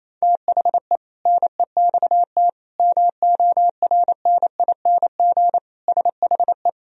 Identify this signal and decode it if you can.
Morse code, heard as a keyed tone, about 28 wpm, 700 Hz. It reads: THE NEXT MORNING SHE
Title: Sentences from Top 1000 Words - Review Part 2 - 28wpm